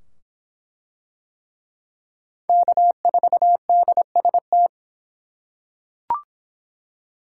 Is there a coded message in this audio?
Yes: Morse code from 2.5 to 4.7 s, heard as a keyed tone, about 24 wpm, 700 Hz. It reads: K4DST